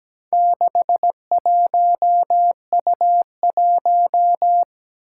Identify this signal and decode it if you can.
Morse code, heard as a keyed tone, about 17 wpm, 700 Hz. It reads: 61U1
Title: Four Challenging Letter-Number Combinations 17wpm